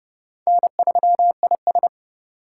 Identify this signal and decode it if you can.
Morse code, heard as a keyed tone, about 29 wpm, 690 Hz. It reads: N3IS